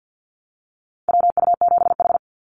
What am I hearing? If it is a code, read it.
Morse code, heard as a keyed tone, about 50 wpm, 700 Hz. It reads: WU7H
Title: NA SOTA Activator Callsigns 50wpm